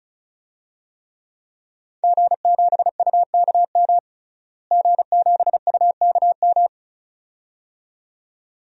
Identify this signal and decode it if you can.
Morse code, heard as a keyed tone, about 36 wpm, 700 Hz. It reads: G7UKM G7UKM
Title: DXCC-100 Call Signs - 5 Characters - Repeated 2x - 35wpm